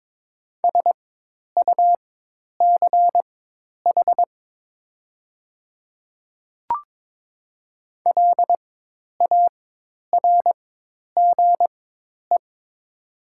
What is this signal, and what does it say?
Morse code, heard as a keyed tone, about 22 wpm, spaced as for 10 wpm, 700 Hz. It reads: SUCH LARGE
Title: 4x Character Spacing - Top 200 Words - 22wpm